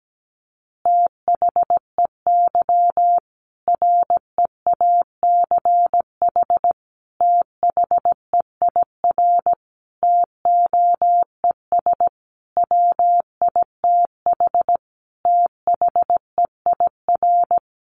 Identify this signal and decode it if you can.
Morse code, heard as a keyed tone, about 17 wpm, 700 Hz. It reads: THEY REACH THEIR TOES WITH THEIR
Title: Easy Phrases 17wpm